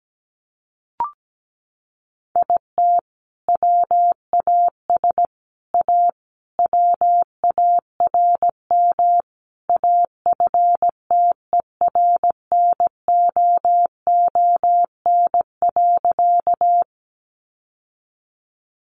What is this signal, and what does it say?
Morse code, heard as a keyed tone, about 17 wpm, 700 Hz. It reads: IT WAS A WARM AFTERNOON.
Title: Sentences from Top 1500 Words 17wpm